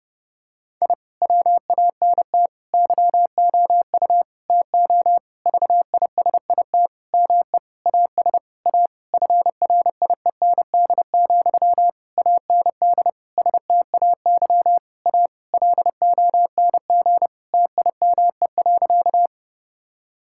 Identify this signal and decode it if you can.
Morse code, heard as a keyed tone, about 30 wpm, 700 Hz. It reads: I WANT YOU TO VISIT ME AS A FRIEND, AND STAY A LONG TIME.
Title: Sentences from Top 1000 Words - Review Part 6 - 30wpm